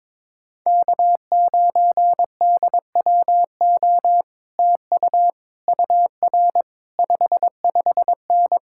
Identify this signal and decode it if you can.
Morse code, heard as a keyed tone, about 22 wpm, 700 Hz. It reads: K9DWO TU UR 55N
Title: Speed-Racing Common QSO Phrases - 22wpm